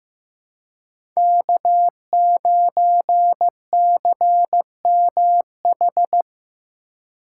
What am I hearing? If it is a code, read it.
Morse code, heard as a keyed tone, about 15 wpm, 700 Hz. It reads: K9CMH
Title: USA Call Signs - 5 Characters - 3x Character Spacing - 15wpm